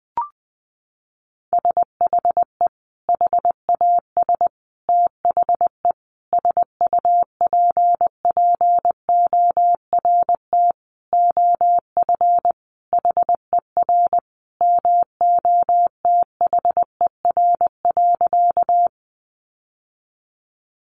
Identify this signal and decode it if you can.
Morse code, heard as a keyed tone, about 20 wpm, 700 Hz. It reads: SHE HAS THE SUPPORT OF HER MOTHER.
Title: Sentences from Top 800 Words 20wpm